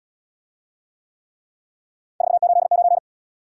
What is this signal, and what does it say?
Morse code, heard as a keyed tone, about 66 wpm, 700 Hz. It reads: 489